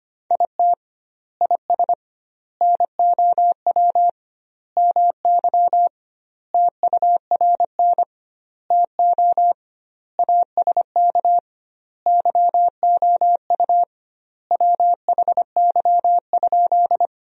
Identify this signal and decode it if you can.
Morse code, heard as a keyed tone, about 25 wpm, 700 Hz. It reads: IT IS NOW MY TURN TO ASK YOU WHY?
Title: 2x Word-Spacing - Sentences from Top 200 Words - 25wpm